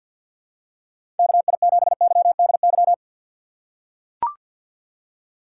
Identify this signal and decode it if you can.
Morse code, heard as a keyed tone, about 45 wpm, 700 Hz. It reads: KI7YDY
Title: USA Call Signs - 2x Character Spacing - 50wpm